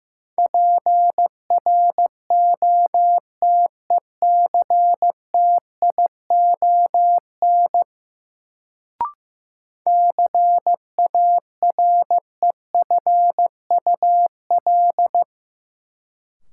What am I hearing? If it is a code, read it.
Morse code, heard as a keyed tone, about 15 wpm, 700 Hz. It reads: PROTECTION CAREFUL
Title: Sentences from Top 1600 Words - New Word Drill Alt - 15wpm